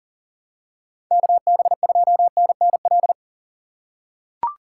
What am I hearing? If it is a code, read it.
Morse code, heard as a keyed tone, about 40 wpm, 700 Hz. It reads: KB2NNL
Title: USA Call Signs - 6 Characters - 4x Character Spacing - 40wpm